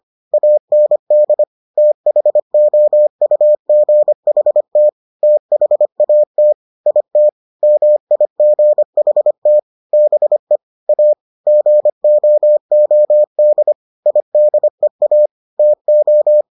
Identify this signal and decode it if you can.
Morse code, heard as a keyed tone, about 25 wpm, 600 Hz. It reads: AND THOUGHT THAT IT MIGHT BE A GOOD IDEA TO